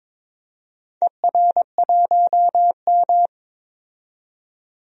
Morse code, heard as a keyed tone, about 22 wpm, 700 Hz. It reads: ER1M
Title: DXCC-100 Call Signs - 4 Characters - 22wpm